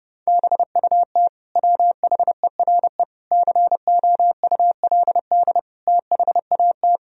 Morse code, heard as a keyed tone, about 32 wpm, 700 Hz. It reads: BUT WHERE COULD THAT
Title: Sentences from Top 1000 Words - Review Part 2 - 30wpm